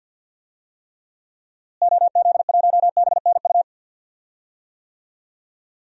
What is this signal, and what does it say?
Morse code, heard as a keyed tone, about 53 wpm, 700 Hz. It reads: OZ1BNU
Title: DXCC-100 Call Signs - 6 Characters - Repeated 2x - 50wpm